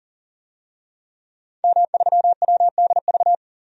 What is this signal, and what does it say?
Morse code, heard as a keyed tone, about 40 wpm, 700 Hz. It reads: M3WDV
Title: DXCC-100 Call Signs - 5 Characters - 40wpm